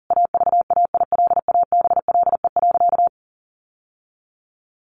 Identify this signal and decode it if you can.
Morse code, heard as a keyed tone, about 40 wpm, 700 Hz. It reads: AVAILABLE.
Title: Sentences from Top 600 Words 40wpm